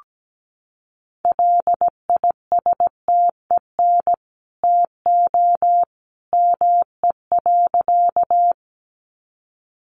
Morse code, heard as a keyed tone, about 17 wpm, 700 Hz. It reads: LISTEN TO ME.
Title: Sentences from Top 400 Words 17wpm